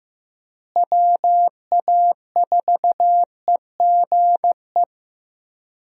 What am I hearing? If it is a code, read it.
Morse code, heard as a keyed tone, about 15 wpm, 700 Hz. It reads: WA4EGE
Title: USA Call Signs - 6 Characters - Repeated 3x - 15wpm